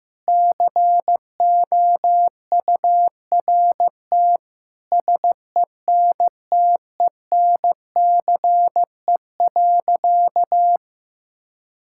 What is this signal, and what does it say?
Morse code, heard as a keyed tone, about 15 wpm, 700 Hz. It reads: COURT SENTENCE.
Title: Sentences from Top 1600 Words 15wpm